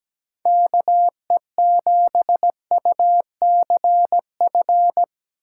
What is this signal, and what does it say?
Morse code, heard as a keyed tone, about 17 wpm, 700 Hz. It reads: KE7UCF
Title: USA Call Signs - 6 Characters - 4x Character Spacing - 17wpm